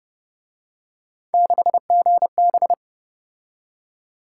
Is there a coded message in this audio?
Yes: Morse code from 1.3 to 2.7 s, heard as a keyed tone, about 28 wpm, 700 Hz. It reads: 6GB